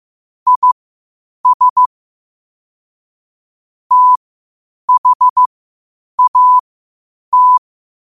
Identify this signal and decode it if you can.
Morse code, heard as a keyed tone, about 15 wpm, spaced as for 8 wpm, 1000 Hz. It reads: IS THAT
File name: qotdfiles / 8wpm